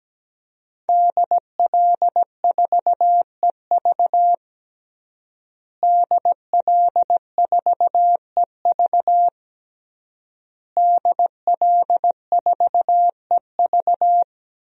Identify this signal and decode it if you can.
Morse code, heard as a keyed tone, about 17 wpm, 700 Hz. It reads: DL4EV DL4EV DL4EV
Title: DXCC-100 Call Signs - 5 Characters - Repeated 3x - 17wpm